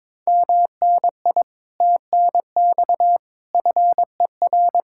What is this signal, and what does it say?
Morse code, heard as a keyed tone, about 22 wpm, 700 Hz. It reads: MNI TNX FER
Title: Speed-Racing Common QSO Phrases - 22wpm